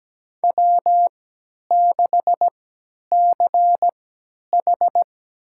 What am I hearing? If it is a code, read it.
Morse code, heard as a keyed tone, about 17 wpm, spaced as for 10 wpm, 700 Hz. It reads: W6CH
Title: USA Call Signs - 3x Character Spacing - 17wpm